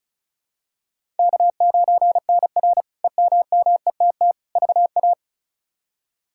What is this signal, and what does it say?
Morse code, heard as a keyed tone, about 36 wpm, 700 Hz. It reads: K9NR EMMETT VA
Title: K1USN SST QSO Practice - 35wpm